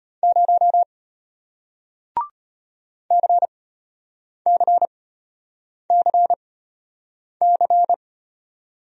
Morse code, heard as a keyed tone, about 34 wpm, spaced as for 8 wpm, 700 Hz. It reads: 0 CCCC